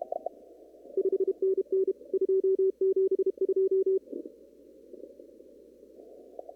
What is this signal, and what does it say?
Morse code, heard as a keyed tone, about 32 wpm, 380 Hz. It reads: 5NN 272
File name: CW / kt5j wpx cw-op-k5pi